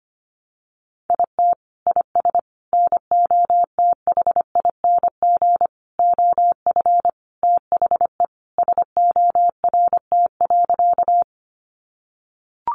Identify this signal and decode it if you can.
Morse code, heard as a keyed tone, about 25 wpm, 700 Hz. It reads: IT IS NOTHING OF THE SORT.
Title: Sentences from Top 1300 Words 25wpm